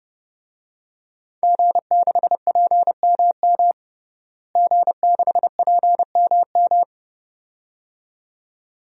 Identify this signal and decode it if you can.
Morse code, heard as a keyed tone, about 30 wpm, 700 Hz. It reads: G6PMM G6PMM
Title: DXCC-100 Call Signs - 5 Characters - Repeated 2x - 30wpm